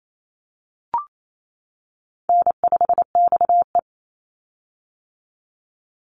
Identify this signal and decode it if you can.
Morse code, heard as a keyed tone, about 28 wpm, 700 Hz. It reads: N5XE
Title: Call Signs - 28wpm